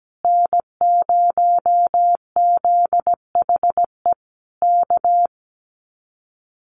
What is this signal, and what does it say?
Morse code, heard as a keyed tone, about 17 wpm, 700 Hz. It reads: N0ZHE K
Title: Common QSO Phrases Encore 17wpm